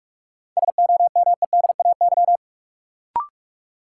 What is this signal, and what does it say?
Morse code, heard as a keyed tone, about 41 wpm, 700 Hz. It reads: SOMEDAY
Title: Sentences from Top 1600 Words - New Word Drill Alt - 45wpm